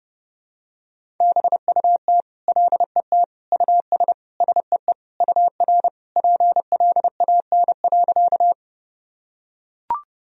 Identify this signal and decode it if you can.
Morse code, heard as a keyed tone, about 30 wpm, 700 Hz. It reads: BUT LET US SEE UR PLAN.
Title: Sentences from Top 500 Words and Abbreviations 30wpm